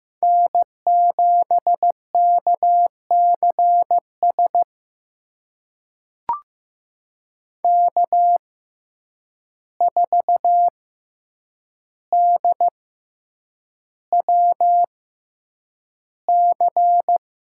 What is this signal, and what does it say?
Morse code, heard as a keyed tone, about 15 wpm, spaced as for 8 wpm, 700 Hz. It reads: N7KCS K4DWC
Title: USA Call Signs - 5 Characters - 6x Character Spacing - 15wpm